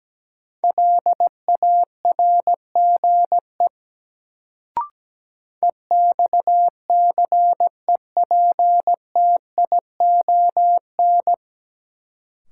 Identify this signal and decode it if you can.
Morse code, heard as a keyed tone, about 17 wpm, 700 Hz. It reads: LARGE EXCEPTION
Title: Sets of 4 Words - New Word Drill - 17wpm